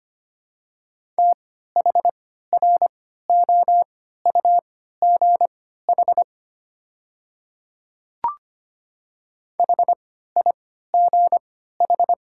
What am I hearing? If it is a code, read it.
Morse code, heard as a keyed tone, about 25 wpm, spaced as for 14 wpm, 700 Hz. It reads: THROUGH HIGH
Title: 3x Character Spacing - Top 200 Words - 25wpm